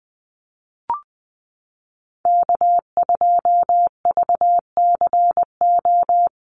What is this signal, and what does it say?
Morse code, heard as a keyed tone, about 20 wpm, 700 Hz. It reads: K2VCO
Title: Call Signs - 20wpm